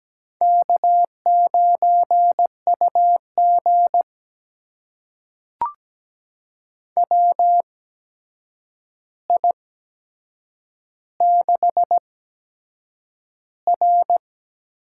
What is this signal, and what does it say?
Morse code, heard as a keyed tone, about 17 wpm, spaced as for 7 wpm, 700 Hz. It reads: K9UG WI6R
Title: USA Call Signs - 4 Characters - 8x Character Spacing - 17wpm